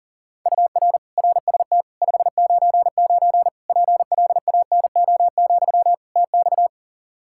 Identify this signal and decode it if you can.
Morse code, heard as a keyed tone, about 40 wpm, 700 Hz. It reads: UR RST 599 PLANO, TX
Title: K1USN SST QSO Practice 40wpm